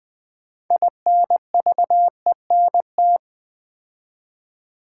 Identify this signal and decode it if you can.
Morse code, heard as a keyed tone, about 20 wpm, 700 Hz. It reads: INVENT
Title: 6 Letter Words - First 3 Letters Given - 20wpm